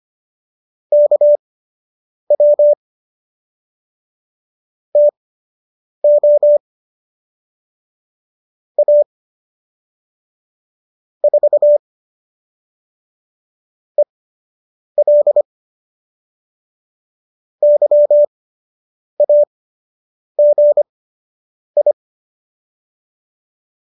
Morse code, heard as a keyed tone, about 25 wpm, spaced as for 8 wpm, 590 Hz. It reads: KW TO A 4 EL YAGI
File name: qso003 08